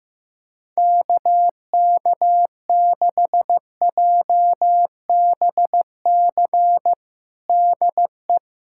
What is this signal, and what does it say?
Morse code, heard as a keyed tone, about 15 wpm, 700 Hz. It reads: KK6JBC DE
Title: USA Callsign DE Callsign - 6 Characters - 15wpm